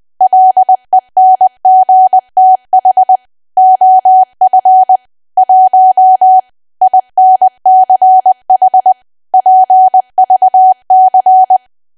Morse code, heard as a keyed tone, about 20 wpm, 750 Hz. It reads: LENGTH OF 1 INCH PVC
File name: cwa / W1AW-Level3 / 20WPM